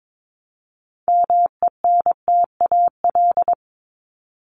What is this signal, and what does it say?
Morse code, heard as a keyed tone, about 22 wpm, 700 Hz. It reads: MENTAL
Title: Top 501-750 Words 22wpm